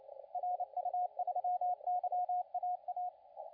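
Morse code, heard as a keyed tone, about 29 wpm, 720 Hz, with radio noise. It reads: RU3YAA
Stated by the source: CW contest audio recorded at LA4C during the SAC CW contest.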